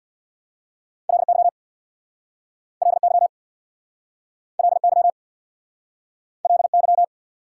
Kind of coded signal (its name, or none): Morse code